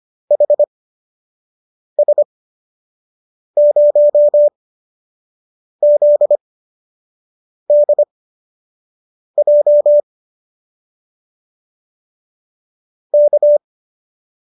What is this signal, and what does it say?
Morse code, heard as a keyed tone, about 25 wpm, spaced as for 6 wpm, 590 Hz. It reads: HS0ZDJ K